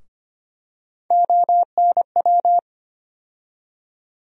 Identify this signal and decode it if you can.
Morse code, heard as a keyed tone, about 25 wpm, 700 Hz. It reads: ONW
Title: USA Call Sign Postfixes - 3 Letters 25wpm